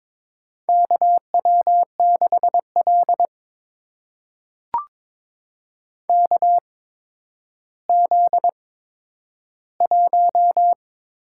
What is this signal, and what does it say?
Morse code, heard as a keyed tone, about 22 wpm, spaced as for 11 wpm, 700 Hz. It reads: KW6L KZ1
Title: USA Call Signs - 4 Characters - 8x Character Spacing - 22wpm